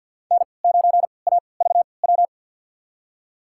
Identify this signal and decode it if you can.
Morse code, heard as a keyed tone, about 53 wpm, spaced as for 29 wpm, 690 Hz. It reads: N9AVW
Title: USA Call Signs - 5 Characters - 3x Character Spacing - 50wpm